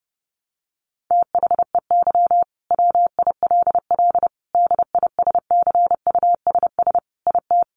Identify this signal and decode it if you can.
Morse code, heard as a keyed tone, about 30 wpm, 700 Hz. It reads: THEY WILL DISCUSS IT
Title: Sentences from Top 1400 Words 30wpm